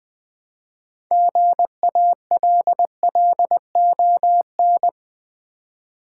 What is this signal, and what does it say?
Morse code, heard as a keyed tone, about 20 wpm, 700 Hz. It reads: GALLON
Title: Build A Word - 3 Letters At A Time - 6-Letter Words - 20wpm